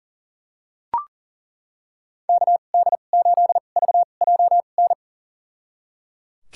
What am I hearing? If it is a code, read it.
Morse code, heard as a keyed tone, about 37 wpm, spaced as for 28 wpm, 700 Hz. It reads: KD8VJN
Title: USA Call Signs - 6 Characters - 2x Character Spacing - 40wpm